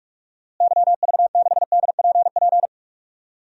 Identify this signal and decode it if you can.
Morse code, heard as a keyed tone, about 45 wpm, 700 Hz. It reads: YV6DPP
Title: DXCC-100 Call Signs - 6 Characters - Repeated 2x - 45wpm